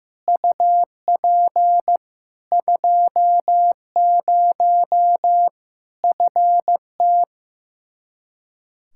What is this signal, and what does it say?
Morse code, heard as a keyed tone, about 15 wpm, 700 Hz. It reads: UP 20 FT
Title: Practice QSO - Exchange 2 - Info - 15wpm